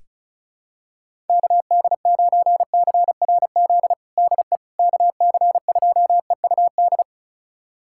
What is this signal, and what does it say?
Morse code, heard as a keyed tone, about 36 wpm, 700 Hz. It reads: KD9CRZ DE KC2EUD
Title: USA Callsign DE Callsign 35wpm